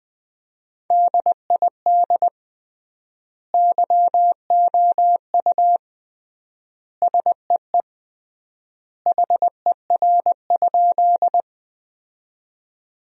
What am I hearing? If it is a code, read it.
Morse code, heard as a keyed tone, about 20 wpm, 700 Hz. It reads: DID YOU SEE HER?
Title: 3x Word-Spacing - Sentences from Top 100 Words - 20wpm